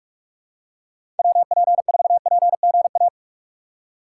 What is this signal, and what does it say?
Morse code, heard as a keyed tone, about 46 wpm, 700 Hz. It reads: WP4PGA
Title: USA Call Signs - 6 Characters - 3x Character Spacing - 45wpm